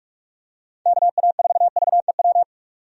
Morse code, heard as a keyed tone, about 46 wpm, 700 Hz. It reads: KA4VEW
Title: USA Call Signs - 6 Characters - 45wpm